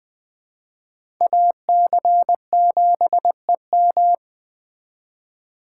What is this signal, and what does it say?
Morse code, heard as a keyed tone, about 20 wpm, 700 Hz. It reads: AC7EM
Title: Speed Racing - USA 2x2 Call Signs 17wpm